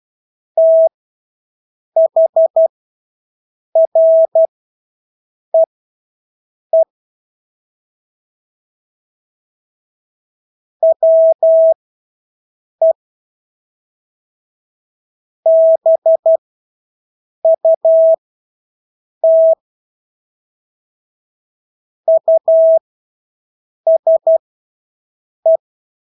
Morse code, heard as a keyed tone, about 12 wpm, spaced as for 6 wpm, 650 Hz. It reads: THREE WE BUT USE
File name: Engelske ord 12-6 wpm